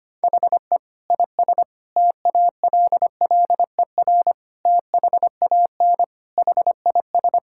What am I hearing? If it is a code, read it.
Morse code, heard as a keyed tone, about 25 wpm, 700 Hz. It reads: HE IS TALLER THAN HIS
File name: practice / 25wpm / Difficult Sentences with Hints - 25wpm